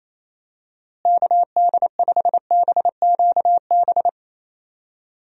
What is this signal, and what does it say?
Morse code, heard as a keyed tone, about 28 wpm, 700 Hz. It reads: KD5BQB
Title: USA Call Signs - 6 Characters - Repeated 2x - 28wpm